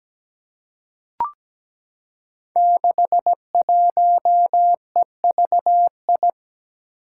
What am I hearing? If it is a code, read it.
Morse code, heard as a keyed tone, about 17 wpm, 700 Hz. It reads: 61EVI
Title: Five Challenging Letter-Number Combinations 17wpm